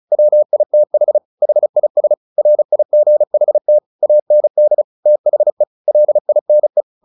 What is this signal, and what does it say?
Morse code, heard as a keyed tone, about 34 wpm, 600 Hz. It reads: WITH HIS RIGHT AND THE LINE
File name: the old man and the sea 2 35wpm